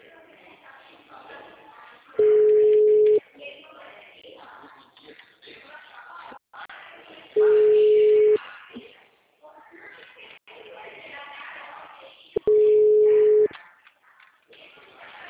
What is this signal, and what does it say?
Ringback tone (Europe, 425 Hz), with speech.